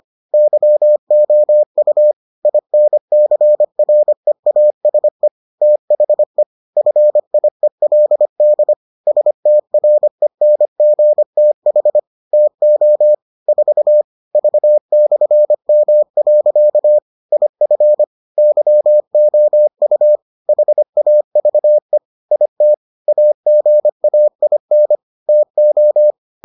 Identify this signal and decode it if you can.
Morse code, heard as a keyed tone, about 25 wpm, 600 Hz. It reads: YOU INCREASE THE FIELD STRENGTH TO 4 V/M. IF YOU HAVE IT AGAIN TO